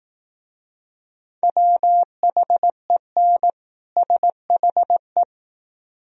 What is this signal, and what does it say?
Morse code, heard as a keyed tone, about 18 wpm, 700 Hz. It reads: WHEN SHE
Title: Speed-Racing - Sets of 2 Words 15wpm